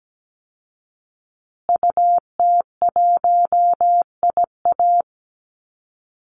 Morse code, heard as a keyed tone, about 17 wpm, 700 Hz. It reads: UT1IA
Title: Call Signs Repeated 3x - 17wpm